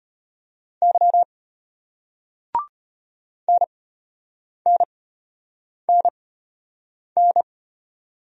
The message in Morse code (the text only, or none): Y NNNN